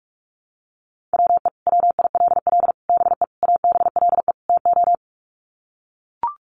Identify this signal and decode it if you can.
Morse code, heard as a keyed tone, about 44 wpm, 700 Hz. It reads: WE WILL BE ABLE TO
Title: Sets of 5 Words Encore 45wpm